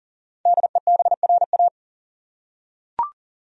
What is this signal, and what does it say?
Morse code, heard as a keyed tone, about 39 wpm, 700 Hz. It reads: DEBRA
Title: USA Common Names - Max 5 Letters - 40wpm